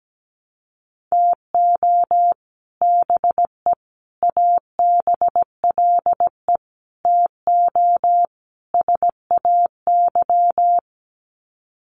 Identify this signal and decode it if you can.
Morse code, heard as a keyed tone, about 17 wpm, 700 Hz. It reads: TO BE ABLE TO SAY
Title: Sets of 5 Words Encore2 17wpm